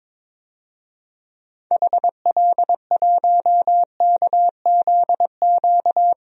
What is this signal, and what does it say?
Morse code, heard as a keyed tone, about 22 wpm, 700 Hz. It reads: HL1KZQ
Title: DXCC-100 Call Signs - 6 Characters - Repeated 2x - 22wpm